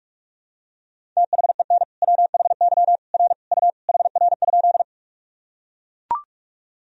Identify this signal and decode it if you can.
Morse code, heard as a keyed tone, about 41 wpm, 700 Hz. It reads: THEN WHY R U HR?
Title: Sentences from Top 200 Words and Abbreviations 45wpm